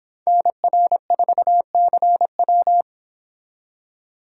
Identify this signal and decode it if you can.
Morse code, heard as a keyed tone, about 25 wpm, 700 Hz. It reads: NR4CW